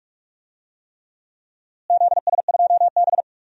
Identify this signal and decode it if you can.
Morse code, heard as a keyed tone, about 43 wpm, 700 Hz. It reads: ZS2B
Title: DXCC-100 Call Signs - 4 Characters - Repeated 3x - 45wpm